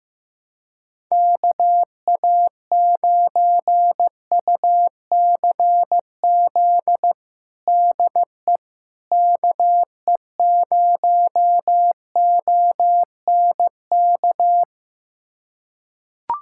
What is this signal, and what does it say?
Morse code, heard as a keyed tone, about 16 wpm, 700 Hz. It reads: KA9UCZ DE KE0ONK
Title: USA Callsign DE Callsign - 6 Characters - 15wpm